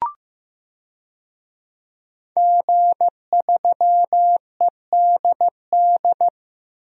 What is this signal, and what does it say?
Morse code, heard as a keyed tone, about 16 wpm, 700 Hz. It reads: G3EDD
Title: DXCC-100 Call Signs - 5 Characters - Repeated 3x - 15wpm